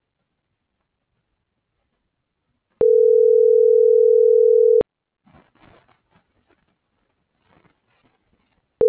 Ringback tone (North America), 440 + 480 Hz.